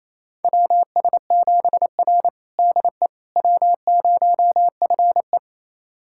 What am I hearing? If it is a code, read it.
Morse code, heard as a keyed tone, about 27 wpm, 700 Hz. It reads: WS7R DE W0FE
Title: USA Callsign DE Callsign - 4 Characters - 28wpm